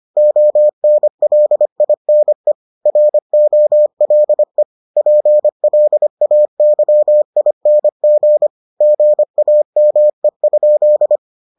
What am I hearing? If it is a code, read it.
Morse code, heard as a keyed tone, about 25 wpm, 600 Hz. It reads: ONLINE ROLE PLAYING GAME?